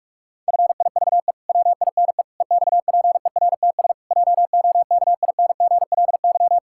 Morse code, heard as a keyed tone, about 47 wpm, 700 Hz. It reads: FIVE WINE EXPERTS JOKINGLY
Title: Pangram Challenge v2 - 45wpm